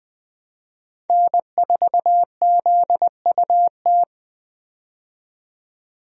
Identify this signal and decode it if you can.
Morse code, heard as a keyed tone, about 21 wpm, 700 Hz. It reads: N4ZUT
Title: USA Call Signs - 5 Characters - 20wpm